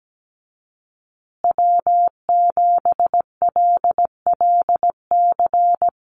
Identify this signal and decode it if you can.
Morse code, heard as a keyed tone, about 17 wpm, 700 Hz. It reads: W7LLC
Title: USA Call Signs Repeated 2x 17wpm